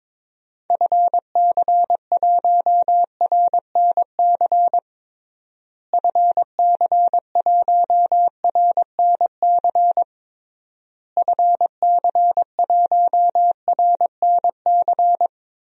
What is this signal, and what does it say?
Morse code, heard as a keyed tone, about 21 wpm, 700 Hz. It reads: FC1RNC FC1RNC FC1RNC
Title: DXCC-100 Call Signs - 6 Characters - Repeated 3x - 22wpm